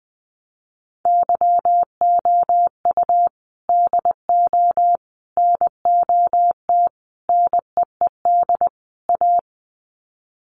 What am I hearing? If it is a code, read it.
Morse code, heard as a keyed tone, about 20 wpm, 700 Hz. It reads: YOU DO NOT NEED A
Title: Sets of 5 Words Encore 20wpm